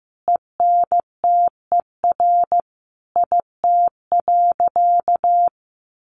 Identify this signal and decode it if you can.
Morse code, heard as a keyed tone, about 14 wpm, 700 Hz. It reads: ENTER IT.
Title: Sentences from Top 1300 Words 15wpm